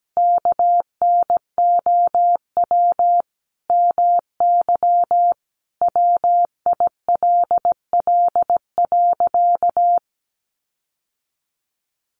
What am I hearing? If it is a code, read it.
Morse code, heard as a keyed tone, about 17 wpm, 700 Hz. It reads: KNOW MY WILL.
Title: Sentences from Top 1500 Words 17wpm